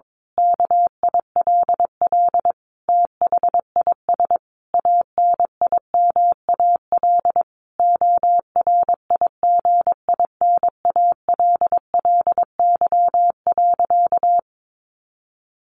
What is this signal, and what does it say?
Morse code, heard as a keyed tone, about 22 wpm, 700 Hz. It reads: KILL THIS ANIMAL ORIGINALLY.
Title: A Sound Of Thunder 22wpm